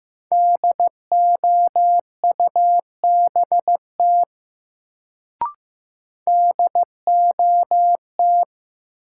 Morse code, heard as a keyed tone, about 15 wpm, 700 Hz. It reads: DOUBT DOT
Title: Sentences from Top 1500 Words - New Word Drill Alt - 15wpm